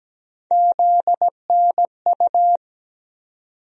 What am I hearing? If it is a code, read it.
Morse code, heard as a keyed tone, about 17 wpm, 700 Hz. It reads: ZNU
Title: USA Call Sign Postfixes - 3 Letters 17wpm